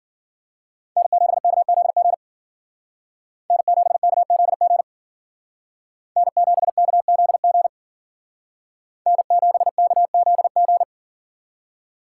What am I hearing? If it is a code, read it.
Morse code, heard as a keyed tone, about 52 wpm, 700 Hz. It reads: N7KZG N7KZG N7KZG N7KZG